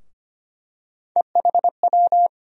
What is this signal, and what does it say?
Morse code, heard as a keyed tone, about 25 wpm, 700 Hz. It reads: EHW
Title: USA Call Sign Postfixes - 3 Letters 25wpm